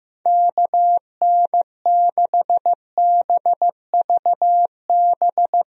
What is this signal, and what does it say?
Morse code, heard as a keyed tone, about 15 wpm, 700 Hz. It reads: KN6BVB
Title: USA Call Signs - 6 Characters - 15wpm